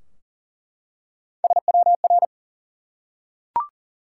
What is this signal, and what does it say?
Morse code, heard as a keyed tone, about 40 wpm, 700 Hz. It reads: SWR
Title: Top 180 QSO Elements 40wpm